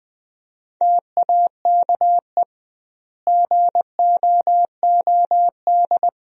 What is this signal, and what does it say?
Morse code, heard as a keyed tone, about 21 wpm, 700 Hz. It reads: TAKE GOOD
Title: 2x Word-Spacing - Sentences from Top 500 Words - 20wpm